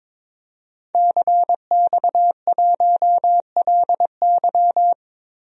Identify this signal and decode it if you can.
Morse code, heard as a keyed tone, about 22 wpm, 700 Hz. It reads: CX1LY
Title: DXCC-100 Call Signs - 5 Characters - Repeated 2x - 22wpm